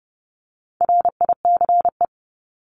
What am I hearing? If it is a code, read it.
Morse code, heard as a keyed tone, about 30 wpm, 700 Hz. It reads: RICE
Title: Top 1501-1750 Words 30wpm